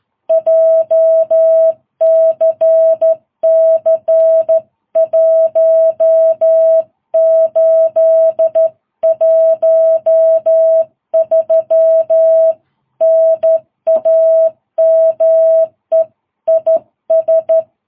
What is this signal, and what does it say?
Morse code, heard as a keyed tone, about 12 wpm, 660 Hz. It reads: JCC 1813 NAME IS
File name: jh8cbh / 2022cw